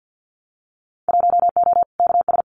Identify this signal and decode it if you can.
Morse code, heard as a keyed tone, about 50 wpm, 700 Hz. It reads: 1O KS
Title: Winter Field Day Exchanges 50wpm